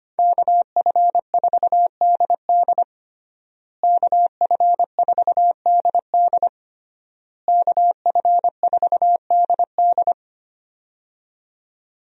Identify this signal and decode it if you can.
Morse code, heard as a keyed tone, about 25 wpm, 700 Hz. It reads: KF4DD KF4DD KF4DD
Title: USA Call Signs - 5 Characters - Repeated 3x - 25wpm